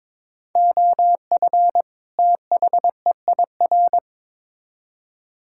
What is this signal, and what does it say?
Morse code, heard as a keyed tone, about 22 wpm, 700 Hz. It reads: OF THEIR
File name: Build A Phrase - 2-Word Phrases - 22wpm